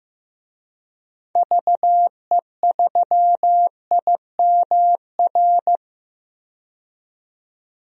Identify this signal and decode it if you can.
Morse code, heard as a keyed tone, about 15 wpm, 700 Hz. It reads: VE3IMR
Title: DXCC-100 Call Signs - 6 Characters - 15wpm